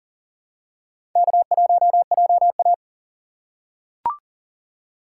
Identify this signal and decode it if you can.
Morse code, heard as a keyed tone, about 37 wpm, 700 Hz. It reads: K1JA
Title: USA Call Signs - 4 Characters - 6x Character Spacing - 40wpm